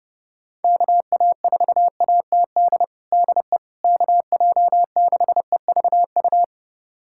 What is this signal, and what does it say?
Morse code, heard as a keyed tone, about 30 wpm, 700 Hz. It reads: KA4ATD DE KJ6EVU
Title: USA Callsign DE Callsign - 6 Characters - 30wpm